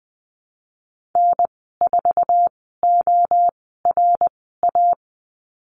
Morse code, heard as a keyed tone, about 20 wpm, spaced as for 14 wpm, 700 Hz. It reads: N4ORA
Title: 082 Lesson 14g - Number 2 - Call Signs 20wpm